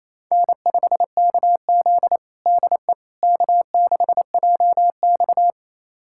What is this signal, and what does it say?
Morse code, heard as a keyed tone, about 28 wpm, 700 Hz. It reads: N5KZ DE K6JX
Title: USA Callsign DE Callsign - 4 Characters - 28wpm